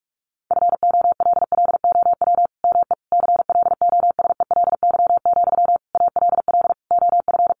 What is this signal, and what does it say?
Morse code, heard as a keyed tone, about 44 wpm, 700 Hz. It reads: FOLLOW ME CLOSELY, ALL OF
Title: Sentences from Top 1400 Words 45wpm